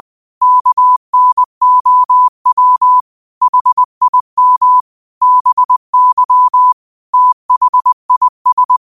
Morse code, heard as a keyed tone, about 20 wpm, 1000 Hz. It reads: KNOW HIM BY THIS
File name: qotdfiles / 20wpm